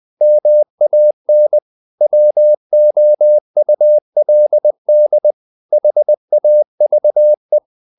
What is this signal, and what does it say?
Morse code, heard as a keyed tone, about 20 wpm, 600 Hz. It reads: MAN WOULD HAVE
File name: the old man and the sea / the old man and the sea part 1 20wpm